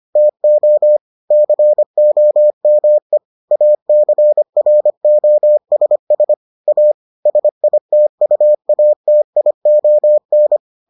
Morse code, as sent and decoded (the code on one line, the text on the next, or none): - --- / -.-. --- -- . / .- -.-. .-. --- ... ... / .- / ... .. - ..- .- - .. --- -.
TO COME ACROSS A SITUATION